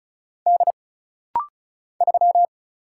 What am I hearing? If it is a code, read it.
Morse code, heard as a keyed tone, about 36 wpm, 700 Hz. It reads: D 3
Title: Single Letter-Number - Rapid-Fire 35wpm